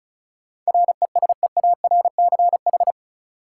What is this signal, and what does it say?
Morse code, heard as a keyed tone, about 36 wpm, 700 Hz. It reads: RESEARCH
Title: Sentences from Top 1000 Words - Review Part 4 - New Word Drill - 35wpm